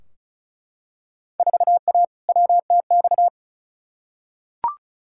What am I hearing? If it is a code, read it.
Morse code, heard as a keyed tone, about 34 wpm, 700 Hz. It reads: 4A WTX
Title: Field Day Exchanges - 35wpm